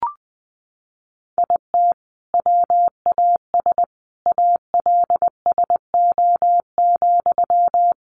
Morse code, heard as a keyed tone, about 20 wpm, 700 Hz. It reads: IT WAS ALSO,
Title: Sentences from Top 1400 Words 20wpm